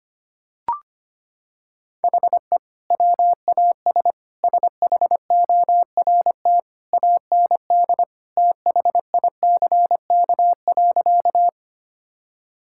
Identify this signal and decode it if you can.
Morse code, heard as a keyed tone, about 25 wpm, 700 Hz. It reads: HE WAS SHORT AND THICK.
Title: Sentences from Top 1600 Words 25wpm